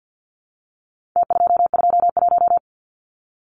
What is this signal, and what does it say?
Morse code, heard as a keyed tone, about 48 wpm, 700 Hz. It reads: T221
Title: Numbers - Contest Style - 4-Digit 50wpm